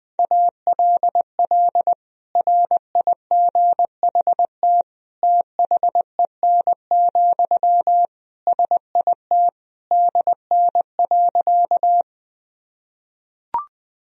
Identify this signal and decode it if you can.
Morse code, heard as a keyed tone, about 20 wpm, 700 Hz. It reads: ALL RIGHT THEN, SIT DN.
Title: Sentences from Top 500 Words and Abbreviations 20wpm